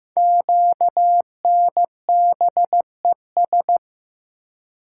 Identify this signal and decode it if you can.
Morse code, heard as a keyed tone, about 15 wpm, 700 Hz. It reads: QNBES
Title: Code Groups - 5 Letters - 15wpm